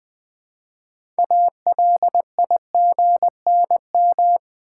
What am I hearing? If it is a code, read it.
Morse code, heard as a keyed tone, about 20 wpm, 700 Hz. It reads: ALIGNM
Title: Build A Word - 3 Letters At A Time - 9-Letter Words - 20wpm